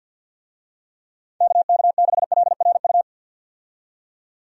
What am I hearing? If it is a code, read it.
Morse code, heard as a keyed tone, about 50 wpm, 700 Hz. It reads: KK6LRU
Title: USA Call Signs - 6 Characters - 50wpm